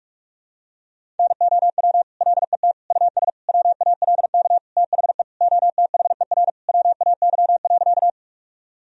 Morse code, heard as a keyed tone, about 46 wpm, 700 Hz. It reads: NOW LET US WALK THE OTHER WAY.
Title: Sentences from Top 1000 Words - Review Part 1 - 45wpm